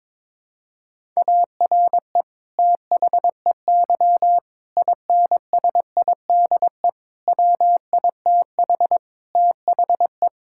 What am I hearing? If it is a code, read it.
Morse code, heard as a keyed tone, about 22 wpm, 700 Hz. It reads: ARE THEY INSIDE WITH THE
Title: Sentences from Top 1000 Words - Review Part 6 - 22wpm